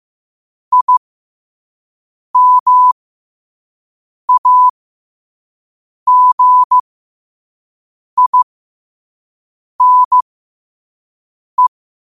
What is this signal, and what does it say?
Morse code, heard as a keyed tone, about 15 wpm, spaced as for 5 wpm, 1000 Hz. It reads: IMAGINE